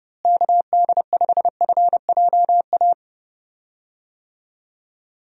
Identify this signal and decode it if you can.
Morse code, heard as a keyed tone, about 30 wpm, 700 Hz. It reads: KD5FJA
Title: USA Call Signs - 6 Characters - Repeated 2x - 30wpm